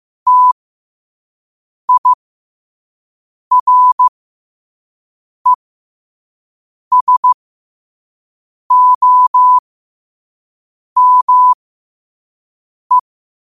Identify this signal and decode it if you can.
Morse code, heard as a keyed tone, about 15 wpm, spaced as for 5 wpm, 1000 Hz. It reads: TIRESOME
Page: New quotes every day in morse code at 5 Words per minute.